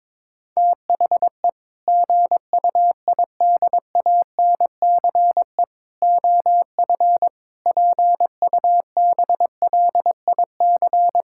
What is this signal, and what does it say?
Morse code, heard as a keyed tone, about 21 wpm, 700 Hz. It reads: THE GUIDANCE OF PUBLIC
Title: Sentences from Top 1600 Words 22wpm